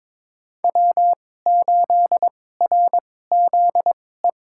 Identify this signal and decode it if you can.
Morse code, heard as a keyed tone, about 22 wpm, spaced as for 16 wpm, 700 Hz. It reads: W8RZE
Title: USA Call Signs - 2x Character Spacing - 22wpm